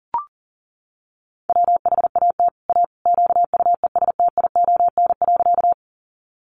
Morse code, heard as a keyed tone, about 40 wpm, 700 Hz. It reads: WHAT A QUESTION.
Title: Sentences from Top 400 Words 40wpm